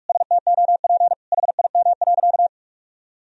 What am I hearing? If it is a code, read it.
Morse code, heard as a keyed tone, about 46 wpm, 700 Hz. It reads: STOP HIM.
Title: Sentences from Top 1000 Words - Review Part 1 - 45wpm